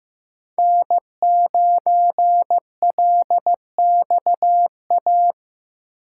Morse code, heard as a keyed tone, about 15 wpm, 700 Hz. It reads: N9LXA
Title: USA Call Signs - 5 Characters - 4x Character Spacing - 15wpm